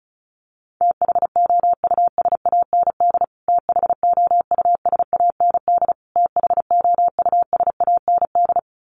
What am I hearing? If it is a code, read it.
Morse code, heard as a keyed tone, about 34 wpm, 700 Hz. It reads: THOUSAND THOUSAND THOUSAND
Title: Top 500 Words Repeated 35wpm